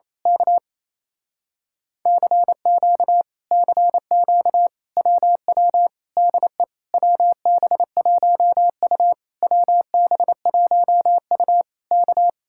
Morse code, heard as a keyed tone, about 27 wpm, 700 Hz. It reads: K CQ CQ WW DE WB1U WB1U K
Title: Speed-Racing Common QSO Phrases - 28wpm